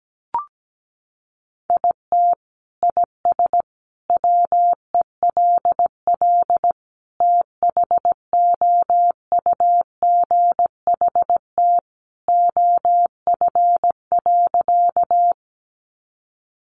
Morse code, heard as a keyed tone, about 17 wpm, 700 Hz. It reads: IT IS WELL THOUGHT OF.
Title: Sentences from Top 300 Words 17wpm